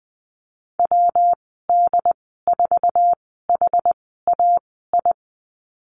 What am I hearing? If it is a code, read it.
Morse code, heard as a keyed tone, about 20 wpm, spaced as for 14 wpm, 700 Hz. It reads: WD4HAI
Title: 122 Lesson 20g - Number 3 - Call Signs 20wpm